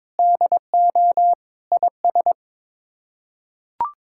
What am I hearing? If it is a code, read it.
Morse code, heard as a keyed tone, about 22 wpm, 700 Hz. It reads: DO IS
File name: practice / 22wpm / Sets of 8 Words - 22wpm